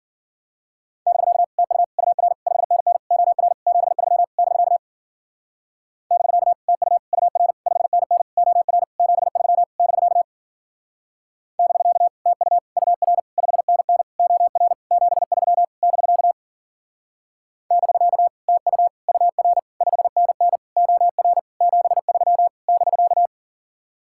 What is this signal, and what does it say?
Morse code, heard as a keyed tone, about 46 wpm, 700 Hz. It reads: <BK> TU UR 5NN OR 73 <BK> <BK> TU UR 5NN OR 73 <BK> <BK> TU UR 5NN OR 73 <BK> <BK> TU UR 5NN OR 73 <BK>
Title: Speed-Racing Common QSO Phrases Encore - 40wpm